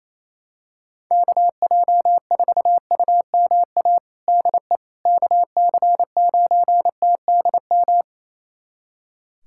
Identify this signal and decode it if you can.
Morse code, heard as a keyed tone, about 28 wpm, 700 Hz. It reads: KJ4UMA DE KC9TDM
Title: USA Callsign DE Callsign - 6 Characters - 28wpm